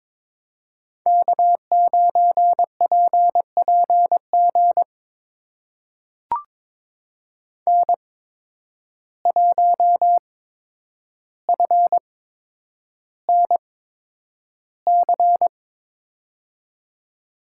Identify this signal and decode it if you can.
Morse code, heard as a keyed tone, about 22 wpm, spaced as for 9 wpm, 700 Hz. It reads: K9PPG N1FNC
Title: USA Call Signs - 5 Characters - 8x Character Spacing - 22wpm